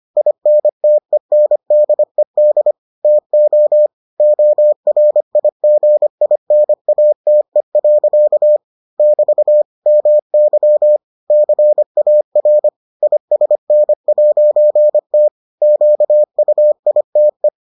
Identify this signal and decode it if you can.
Morse code, heard as a keyed tone, about 25 wpm, 600 Hz. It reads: INTENDED TO ORIGINATE. <BT> MY CAR ISN'T QUITE